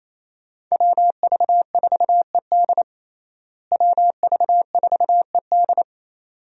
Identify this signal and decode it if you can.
Morse code, heard as a keyed tone, about 28 wpm, 700 Hz. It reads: WV4ED WV4ED
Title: USA Call Signs - 5 Characters - Repeated 2x - 28wpm